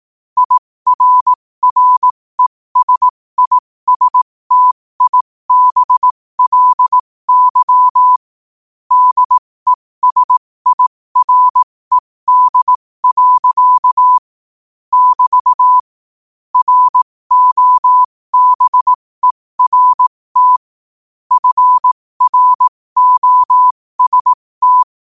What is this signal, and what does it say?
Morse code, heard as a keyed tone, about 18 wpm, spaced as for 15 wpm, 1000 Hz. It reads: IRRESISTIBLY DESIRED. <BT> ROBERT FROST
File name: qotdfiles / 15wpm